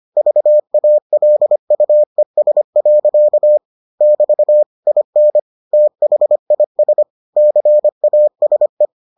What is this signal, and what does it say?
Morse code, heard as a keyed tone, about 25 wpm, 600 Hz. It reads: VALUES. <BT> IN THIS CASE